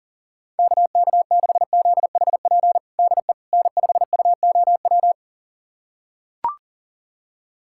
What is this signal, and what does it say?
Morse code, heard as a keyed tone, about 40 wpm, 700 Hz. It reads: KK6ZHP DE N5UOW
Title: USA Callsign DE Callsign 40wpm